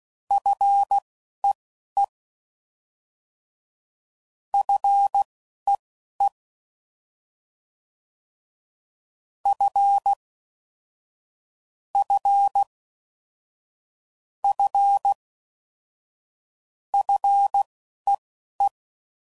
Morse code, heard as a keyed tone, about 16 wpm, spaced as for 11 wpm, 800 Hz. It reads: FEE FEE F F F FEE
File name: Prefixes / FLs & THs / F Prefixes